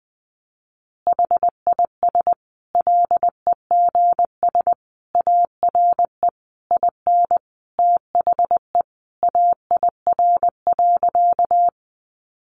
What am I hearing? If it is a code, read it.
Morse code, heard as a keyed tone, about 20 wpm, 700 Hz. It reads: HIS LEGS ARE IN THE AIR.
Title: Additional Easy Phrases 20wpm